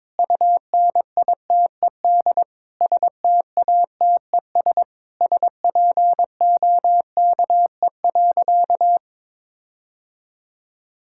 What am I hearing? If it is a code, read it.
Morse code, heard as a keyed tone, about 22 wpm, 700 Hz. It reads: UNITED STATES SPOKE.
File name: practice / 22wpm / Sentences with 4-Word Phrases - 22wpm